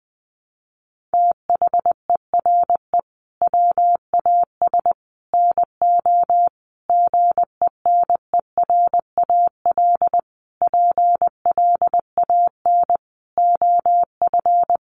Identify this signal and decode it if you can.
Morse code, heard as a keyed tone, about 20 wpm, 700 Hz. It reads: THERE WAS NO GENERAL PLAN OF
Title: Sentences from Top 1200 Words 20wpm